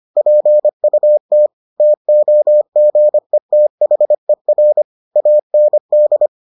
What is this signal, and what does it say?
Morse code, heard as a keyed tone, about 25 wpm, 600 Hz. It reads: PUT TOGETHER AND